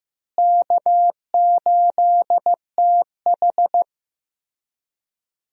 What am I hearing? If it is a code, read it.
Morse code, heard as a keyed tone, about 15 wpm, 700 Hz. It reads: K8TH
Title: USA Call Signs - 4 Characters - Repeated 3x - 15wpm